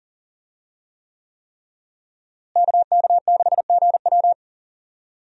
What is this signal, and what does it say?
Morse code, heard as a keyed tone, about 40 wpm, 700 Hz. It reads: KK6GW
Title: USA Call Signs - 5 Characters - Repeated 3x - 40wpm